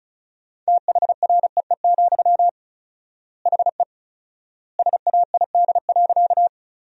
Morse code, heard as a keyed tone, about 36 wpm, 700 Hz. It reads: THREE, HE SAID.
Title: 4x Word-Spacing - Sentences from Top 200 Words - 35wpm